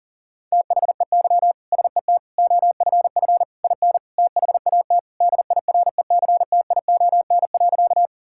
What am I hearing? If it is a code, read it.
Morse code, heard as a keyed tone, about 40 wpm, 700 Hz. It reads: THEY SET OFF IN THAT DIRECTION.
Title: Sentences from Top 1000 Words - Review Part 5 - 40wpm